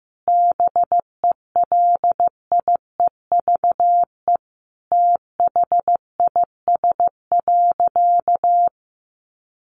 Morse code, heard as a keyed tone, about 15 wpm, 700 Hz. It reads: BELIEVE THIS.
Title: Sentences from Top 1300 Words 15wpm